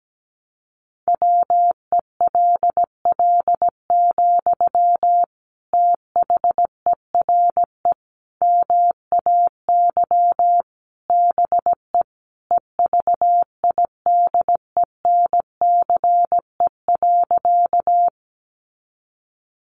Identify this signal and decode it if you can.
Morse code, heard as a keyed tone, about 17 wpm, 700 Hz. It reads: WELL, THERE MAY BE EVIDENCE.
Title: Sentences from Top 1300 Words 17wpm